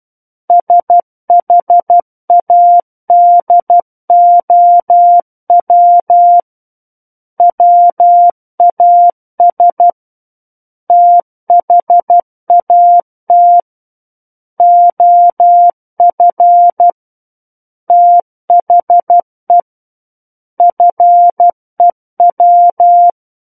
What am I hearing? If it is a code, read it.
Morse code, heard as a keyed tone, about 12 wpm, 700 Hz. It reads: SHADOW WAS THAT OF THE FEW
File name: Book One / 03-Chapter 3 - 12 WPM